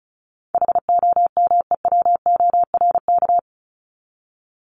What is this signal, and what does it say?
Morse code, heard as a keyed tone, about 34 wpm, 700 Hz. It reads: HOMEWORK
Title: Top 1501-1750 Words 35wpm